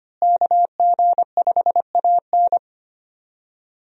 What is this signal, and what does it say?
Morse code, heard as a keyed tone, about 25 wpm, 700 Hz. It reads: KG5AN
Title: USA Call Signs - 5 Characters - Repeated 2x - 25wpm